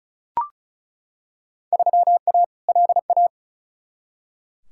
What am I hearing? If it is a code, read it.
Morse code, heard as a keyed tone, about 34 wpm, 700 Hz. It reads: 3A LA
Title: Field Day Exchanges - 35wpm